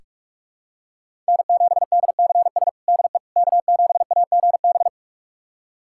Morse code, heard as a keyed tone, about 46 wpm, 700 Hz. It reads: N7DCS DE K7AGB
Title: USA Callsign DE Callsign 45wpm